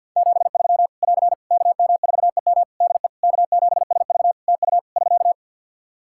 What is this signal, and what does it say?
Morse code, heard as a keyed tone, about 52 wpm, 700 Hz. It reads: 73 <AR> KM4EM DE K7SV TU <SK>
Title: Practice QSO - Exchange 4 - Ending - 50wpm